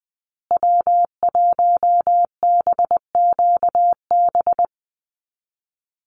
Morse code, heard as a keyed tone, about 20 wpm, 700 Hz. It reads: W1BQB
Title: USA Call Signs Repeated 2x 20wpm